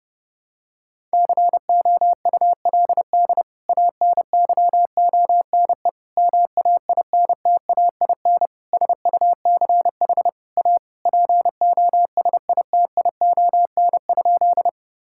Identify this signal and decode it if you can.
Morse code, heard as a keyed tone, about 32 wpm, 700 Hz. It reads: COULD ANYONE MAINTAIN SUCH A POSITION?
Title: Sentences from Top 1000 Words - Review Part 5 - 30wpm